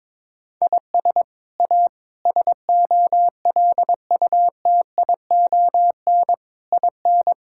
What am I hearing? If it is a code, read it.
Morse code, heard as a keyed tone, about 22 wpm, 700 Hz. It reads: IS A SOLUTION IN
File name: practice / 22wpm / Sets of 8 Words - 22wpm